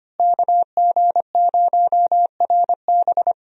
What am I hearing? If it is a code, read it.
Morse code, heard as a keyed tone, about 25 wpm, 700 Hz. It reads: KG0RB
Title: USA Call Signs - 5 Characters - Repeated 3x - 25wpm